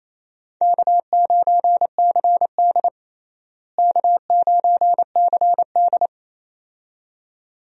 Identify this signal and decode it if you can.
Morse code, heard as a keyed tone, about 28 wpm, 700 Hz. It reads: K9CD K9CD
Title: USA Call Signs - 4 Characters - Repeated 2x - 28wpm